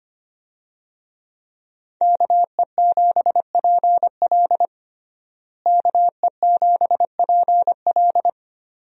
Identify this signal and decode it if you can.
Morse code, heard as a keyed tone, about 25 wpm, 700 Hz. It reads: KE7PL KE7PL
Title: USA Call Signs - 5 Characters - Repeated 2x - 25wpm